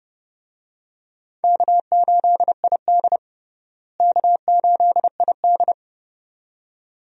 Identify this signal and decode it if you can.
Morse code, heard as a keyed tone, about 30 wpm, 700 Hz. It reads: K8ID K8ID
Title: USA Call Signs - 4 Characters - Repeated 2x - 30wpm